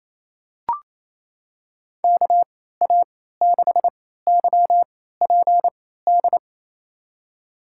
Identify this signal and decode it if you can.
Morse code, heard as a keyed tone, about 28 wpm, spaced as for 16 wpm, 700 Hz. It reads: KA6YPD
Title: USA Call Signs - 6 Characters - 3x Character Spacing - 28wpm